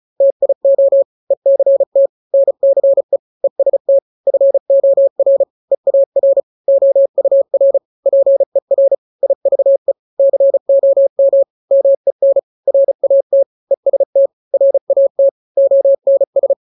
Morse code, heard as a keyed tone, about 34 wpm, 550 Hz. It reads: TIO ECT NCE EST FOR EAR OUR PER IVE COM MEN RAT EST RAT ONS
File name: 3-grams 35 30 WPM